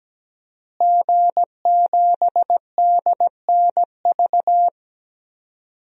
Morse code, heard as a keyed tone, about 17 wpm, 700 Hz. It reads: G7DNV
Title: DXCC-100 Call Signs - 5 Characters - Repeated 3x - 17wpm